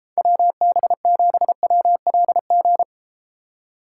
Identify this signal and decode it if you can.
Morse code, heard as a keyed tone, about 32 wpm, 700 Hz. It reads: WB7WLG